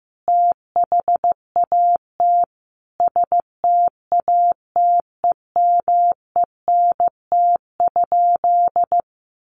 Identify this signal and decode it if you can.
Morse code, heard as a keyed tone, about 15 wpm, 700 Hz. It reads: THAT STATEMENT?
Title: Sentences from Top 1200 Words 15wpm